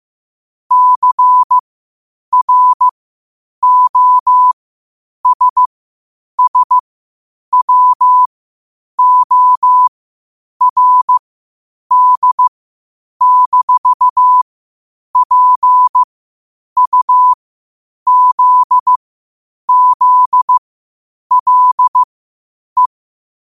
Morse code, heard as a keyed tone, about 15 wpm, spaced as for 8 wpm, 1000 Hz. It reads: CROSSWORD-PUZZLE
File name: qotdfiles / 8wpm